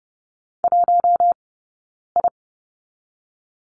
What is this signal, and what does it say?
Morse code, heard as a keyed tone, about 30 wpm, 700 Hz. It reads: 1 I
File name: 092 Lesson 16d - Letter c - Two Characters 30wpm